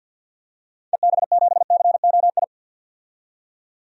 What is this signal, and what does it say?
Morse code, heard as a keyed tone, about 52 wpm, 700 Hz. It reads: EB7COI
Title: DXCC-100 Call Signs - 6 Characters - 50wpm